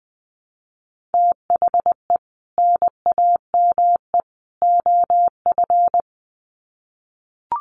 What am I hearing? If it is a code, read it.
Morse code, heard as a keyed tone, about 19 wpm, 700 Hz. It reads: THE NAME OF
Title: Sets of 3 Words 20wpm